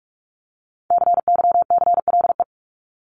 Morse code, heard as a keyed tone, about 44 wpm, 700 Hz. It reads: CYCLE
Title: Top 751-1000 Words 45wpm